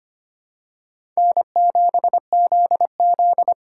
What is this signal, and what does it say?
Morse code, heard as a keyed tone, about 25 wpm, 700 Hz. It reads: N7ZZ
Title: USA Call Signs - 3x Character Spacing - 25wpm